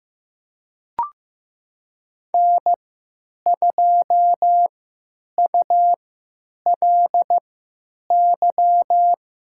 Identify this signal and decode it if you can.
Morse code, heard as a keyed tone, about 16 wpm, spaced as for 9 wpm, 700 Hz. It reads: N2ULY
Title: USA Call Signs - 5 Characters - 3x Character Spacing - 15wpm